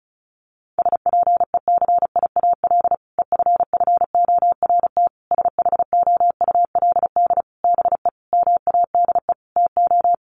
Morse code, heard as a keyed tone, about 34 wpm, 700 Hz. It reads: SPECIAL EFFORT SHOULD BE MADE TO
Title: Sentences from Top 1200 Words 35wpm